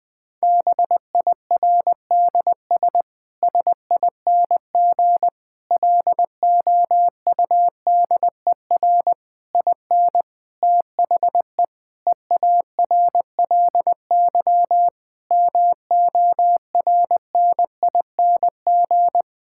Morse code, heard as a keyed tone, about 20 wpm, 700 Hz. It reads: BIRDS SING LOUDER IN THE EARLY MORNING
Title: Difficult Sentences with Hints - 20wpm